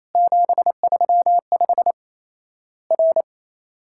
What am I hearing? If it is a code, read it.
Morse code, heard as a keyed tone, about 28 wpm, 700 Hz. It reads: 735 R
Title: Worked All Europe DX Contest - QTC Exchange - 28wpm